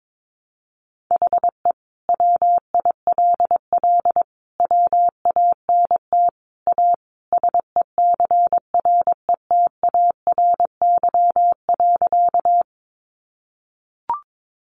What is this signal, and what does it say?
Morse code, heard as a keyed tone, about 22 wpm, 700 Hz. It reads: HE WILL WANT A SECRETARY.
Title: Sentences from Top 1300 Words 22wpm